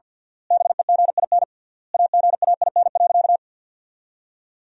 Morse code, heard as a keyed tone, about 50 wpm, 700 Hz. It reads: BEGIN AGAIN.
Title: 3x Word-Spacing - Sentences from Top 300 Words - 50wpm